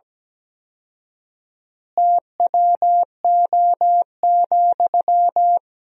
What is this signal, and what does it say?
Morse code, heard as a keyed tone, about 17 wpm, 700 Hz. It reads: TWO,
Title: 4x Word-Spacing - Sentences from Top 200 Words - 17wpm